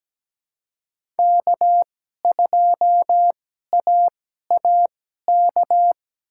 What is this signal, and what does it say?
Morse code, heard as a keyed tone, about 17 wpm, spaced as for 12 wpm, 700 Hz. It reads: K2AAK
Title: USA Call Signs - 5 Characters - 2x Character Spacing - 17wpm